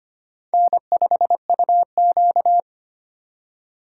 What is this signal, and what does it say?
Morse code, heard as a keyed tone, about 31 wpm, 700 Hz. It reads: N5UQ